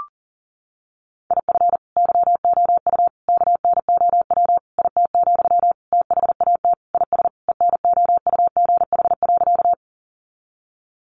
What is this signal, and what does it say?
Morse code, heard as a keyed tone, about 40 wpm, 700 Hz. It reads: IF YOU KNOW IT, THAT IS ENOUGH.
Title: Sentences from Top 400 Words 40wpm